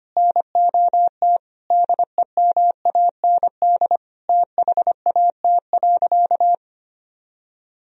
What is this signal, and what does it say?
Morse code, heard as a keyed tone, about 25 wpm, 700 Hz. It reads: NOT DEMAND THAT.
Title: Sentences from Top 1000 Words - Review Part 4 - 25wpm